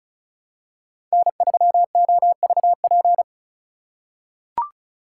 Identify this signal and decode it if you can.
Morse code, heard as a keyed tone, about 36 wpm, 700 Hz. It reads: N3OVP
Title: USA Call Signs - 3x Character Spacing - 35wpm